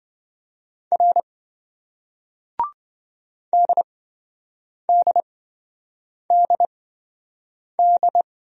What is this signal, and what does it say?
Morse code, heard as a keyed tone, about 26 wpm, spaced as for 7 wpm, 700 Hz. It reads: R DDDD